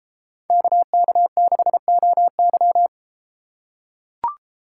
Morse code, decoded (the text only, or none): KK6OY